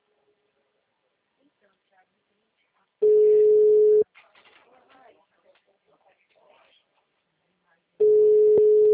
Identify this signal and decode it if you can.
Ringback tone (Europe, 425 Hz), with speech.